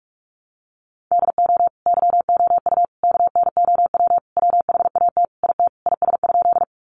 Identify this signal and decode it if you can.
Morse code, heard as a keyed tone, about 44 wpm, 700 Hz. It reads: DO YOU KNOW WHAT IT IS?
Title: Sentences from Top 100 Words 45wpm